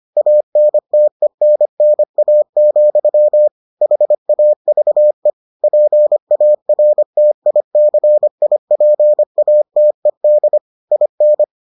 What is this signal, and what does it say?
Morse code, heard as a keyed tone, about 25 wpm, 600 Hz. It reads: ANTENNA, HAVE PARTICIPATED IN